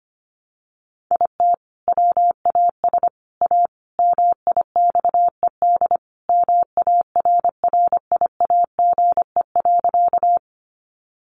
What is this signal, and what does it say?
Morse code, heard as a keyed tone, about 24 wpm, 700 Hz. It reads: IT WAS A MIXED MARRIAGE.
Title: Sentences from Top 1500 Words 25wpm